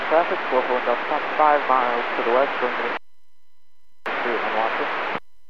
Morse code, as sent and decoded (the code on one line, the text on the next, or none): none